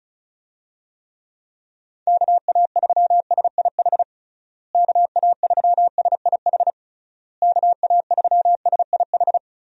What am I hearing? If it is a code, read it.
Morse code, heard as a keyed tone, about 36 wpm, 700 Hz. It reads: KA3SIH KA3SIH KA3SIH
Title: USA Call Signs - 6 Characters - Repeated 3x - 35wpm